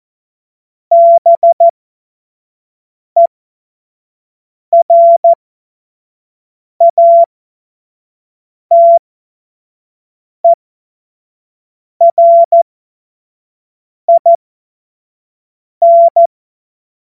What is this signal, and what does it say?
Morse code, heard as a keyed tone, about 14 wpm, spaced as for 5 wpm, 680 Hz. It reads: BERATERIN
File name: Morsekurs / Witze / Witz 07. Februar 2023 25 F70